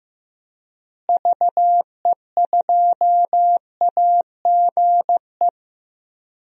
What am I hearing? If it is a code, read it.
Morse code, heard as a keyed tone, about 15 wpm, 700 Hz. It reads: VE2AGE
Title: DXCC-100 Call Signs - 6 Characters - Repeated 3x - 15wpm